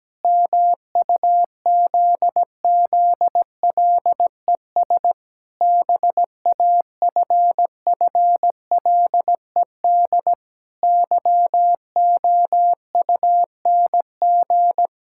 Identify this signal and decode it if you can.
Morse code, heard as a keyed tone, about 17 wpm, 700 Hz. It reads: MUZZLES BAFFLED YOUNG
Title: Pangram Challenge v2 - 17wpm